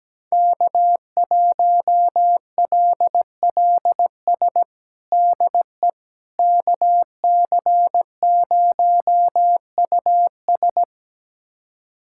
Morse code, heard as a keyed tone, about 17 wpm, 700 Hz. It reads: K1LLS DE KC0US
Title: USA Callsign DE Callsign - 5 Characters - 17wpm